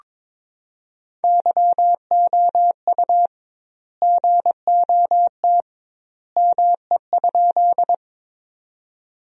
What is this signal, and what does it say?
Morse code, heard as a keyed tone, about 22 wpm, 700 Hz. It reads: YOU GOT ME?
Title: 2x Word-Spacing - Sentences from Top 300 Words - 22wpm